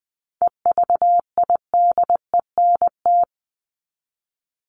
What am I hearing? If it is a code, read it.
Morse code, heard as a keyed tone, about 20 wpm, 700 Hz. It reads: EVIDENT
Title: LY Suffix - Top 100 Words with Base Words 20wpm